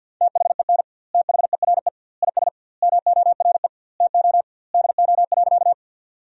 Morse code, heard as a keyed tone, about 50 wpm, 700 Hz. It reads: THEN THERE IS MORE TO DO.
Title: 2x Word-Spacing - Sentences from Top 100 Words - 50wpm